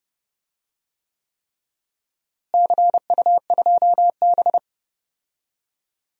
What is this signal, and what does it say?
Morse code, heard as a keyed tone, about 30 wpm, 700 Hz. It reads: CU2B
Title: DXCC-100 Call Signs - 4 Characters - Repeated 3x - 30wpm